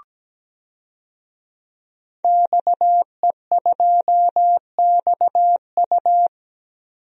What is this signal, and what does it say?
Morse code, heard as a keyed tone, about 17 wpm, 700 Hz. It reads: XE2XU
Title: DXCC-100 Call Signs - 5 Characters - 17wpm